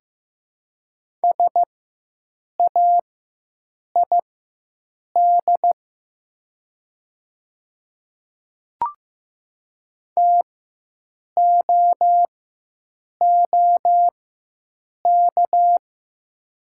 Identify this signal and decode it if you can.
Morse code, heard as a keyed tone, about 15 wpm, spaced as for 7 wpm, 700 Hz. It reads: SAID TOOK
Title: 4x Character Spacing - Top 300 Words - 15wpm